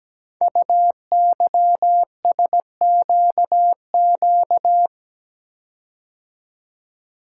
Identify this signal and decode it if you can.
Morse code, heard as a keyed tone, about 17 wpm, 700 Hz. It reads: UYSQQ
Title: Code Groups - 5 Letters - 17wpm